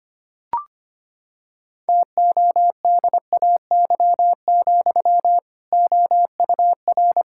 Morse code, heard as a keyed tone, about 25 wpm, 700 Hz. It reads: TODAY, OUR
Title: Sentences from Top 1600 Words 25wpm